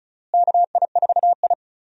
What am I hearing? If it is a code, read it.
Morse code, heard as a keyed tone, about 36 wpm, 700 Hz. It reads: KI4I
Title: USA Call Signs - 4 Characters - Repeated 2x - 35wpm